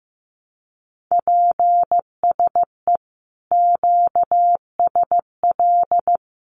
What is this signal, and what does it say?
Morse code, heard as a keyed tone, about 14 wpm, 700 Hz. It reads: PSE QSL
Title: Common QSO Phrases Encore 15wpm